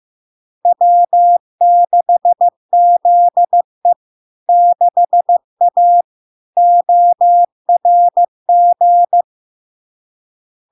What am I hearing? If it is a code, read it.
Morse code, heard as a keyed tone, about 15 wpm, 700 Hz. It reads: W6ZE 6A ORG
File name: LARK-FD-2022-Log-in-CW-15wpm-700-Hz